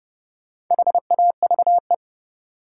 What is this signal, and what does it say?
Morse code, heard as a keyed tone, about 30 wpm, 700 Hz. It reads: HAVE
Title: Sentences from Top 100 Words - New Word Drill - 30wpm